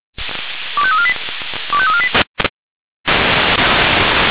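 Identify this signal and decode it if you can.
Five-tone selective call (ZVEI), heard as quick runs of five tones, tones of about 80 ms. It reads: 25348 25348
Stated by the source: ZVEI five-tone selective call received on an SDR.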